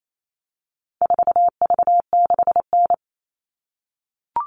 Morse code, heard as a keyed tone, about 28 wpm, 700 Hz. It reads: 4V6N
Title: All Cut Numbers - 4-Digit 28wpm